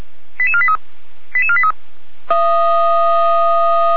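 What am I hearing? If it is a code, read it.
Five-tone selective call (ZVEI), heard as quick runs of five tones, tones of about 70 ms. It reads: 88483 88483
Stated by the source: ZVEI five-tone selective call received by SDR.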